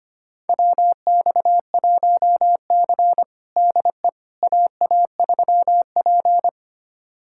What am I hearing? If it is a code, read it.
Morse code, heard as a keyed tone, about 25 wpm, 700 Hz. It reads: WX1C DE AA3P
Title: USA Callsign DE Callsign - 4 Characters - 25wpm